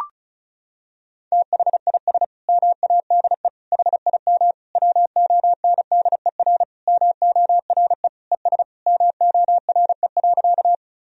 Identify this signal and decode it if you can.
Morse code, heard as a keyed tone, about 34 wpm, 700 Hz. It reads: THIS MADE HIM WONDER MORE ES MORE.
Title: Sentences from Top 500 Words and Abbreviations 35wpm